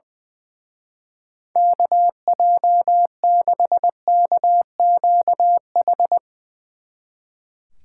Morse code, heard as a keyed tone, about 20 wpm, 700 Hz. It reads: KJ6KQH
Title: Speed Racing - USA 2x3 Call Signs 20wpm